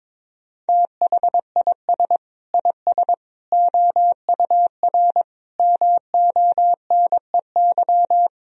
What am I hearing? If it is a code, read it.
Morse code, heard as a keyed tone, about 22 wpm, 700 Hz. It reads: THIS IS OUR MONEY
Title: Sentences from Top 1000 Words - Review Part 2 - 22wpm